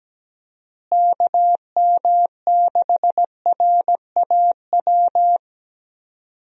Morse code, heard as a keyed tone, about 17 wpm, 700 Hz. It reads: KM6RAW
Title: USA Call Signs - 6 Characters - Repeated 3x - 17wpm